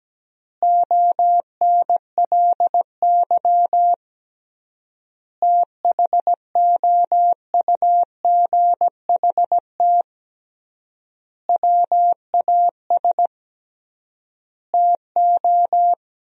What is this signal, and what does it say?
Morse code, heard as a keyed tone, about 17 wpm, 700 Hz. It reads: ONLY THOUGHT WAS TO
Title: 3x Word-Spacing - Sentences from Top 400 Words - 17wpm